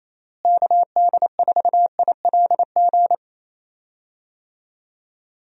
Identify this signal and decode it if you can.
Morse code, heard as a keyed tone, about 28 wpm, 700 Hz. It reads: KD4ILG
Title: USA Call Signs - 6 Characters - Repeated 2x - 28wpm